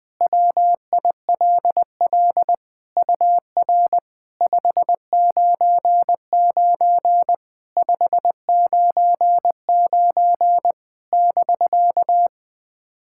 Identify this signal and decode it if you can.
Morse code, heard as a keyed tone, about 20 wpm, 700 Hz. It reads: WILL UR 599 599 <BK>
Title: SOTA QSO Practice 20wpm